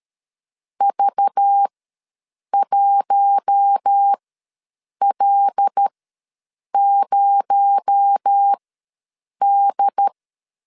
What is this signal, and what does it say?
Morse code, heard as a keyed tone, about 13 wpm, spaced as for 7 wpm, 800 Hz. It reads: V1L0D
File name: LEZIONI / 7 GF9Z / 7 G ws 12wpm 3140